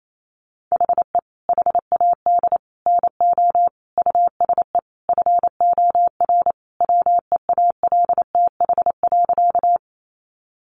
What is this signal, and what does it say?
Morse code, heard as a keyed tone, about 28 wpm, 700 Hz. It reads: HE HAD NO USE FOR WEALTH.
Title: Sentences from Top 1200 Words 28wpm